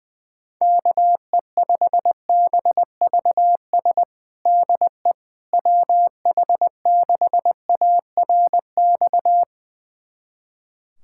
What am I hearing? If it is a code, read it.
Morse code, heard as a keyed tone, about 21 wpm, 700 Hz. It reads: KE5BVS DE WH6ARX
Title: USA Callsign DE Callsign - 6 Characters - 20wpm